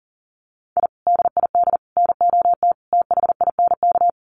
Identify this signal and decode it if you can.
Morse code, heard as a keyed tone, about 40 wpm, 700 Hz. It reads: I DID NOT THINK
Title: Sets of 4 Words 40wpm